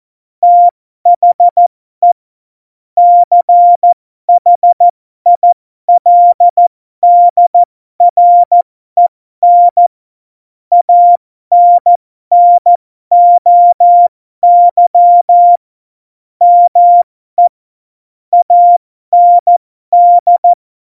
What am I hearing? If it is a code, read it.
Morse code, heard as a keyed tone, about 14 wpm, spaced as for 12 wpm, 700 Hz. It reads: THE CHILDREN ANNOY ME AND
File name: Morsekurs / Witze